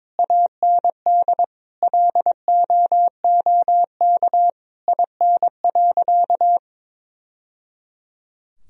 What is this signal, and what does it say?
Morse code, heard as a keyed tone, about 22 wpm, 700 Hz. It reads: AND LOOK IN.
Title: Sentences from Top 1000 Words - Review Part 2 - 22wpm